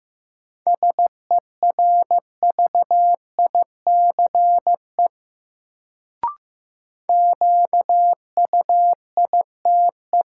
Morse code, heard as a keyed tone, about 15 wpm, 700 Hz. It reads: SERVICE QUITE
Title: Sentences from Top 700 Words - New Word Drill - 15wpm